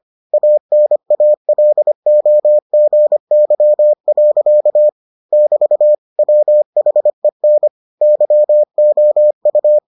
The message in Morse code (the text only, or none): ANALOGY. <BT> WHEN YOU